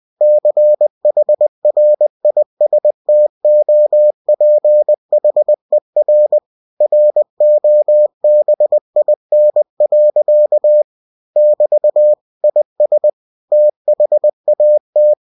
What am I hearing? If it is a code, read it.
Morse code, heard as a keyed tone, about 20 wpm, 600 Hz. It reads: CHRISTOPHER ROBIN. <BT> IS THAT